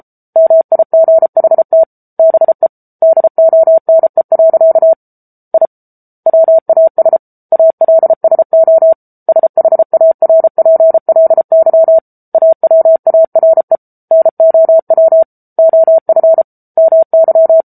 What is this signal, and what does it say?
Morse code, heard as a keyed tone, about 34 wpm, 650 Hz. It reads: MIGHT BE DONE. I WAS ALSO SHARPLY AWARE NOW OF MY
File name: Book Two / 07-Chapter 7 - 33 WPM